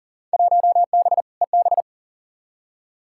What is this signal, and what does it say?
Morse code, heard as a keyed tone, about 37 wpm, 700 Hz. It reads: 1B EB
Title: Field Day Exchanges Encore - 40wpm